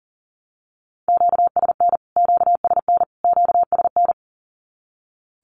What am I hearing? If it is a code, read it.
Morse code, heard as a keyed tone, about 40 wpm, 700 Hz. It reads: QSN QSN QSN
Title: Q-Codes Repeated 3x 40wpm